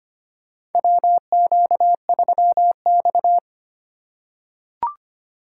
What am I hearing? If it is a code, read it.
Morse code, heard as a keyed tone, about 25 wpm, 700 Hz. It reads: WQ3X
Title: USA Call Signs - 4 Characters - 3x Character Spacing - 25wpm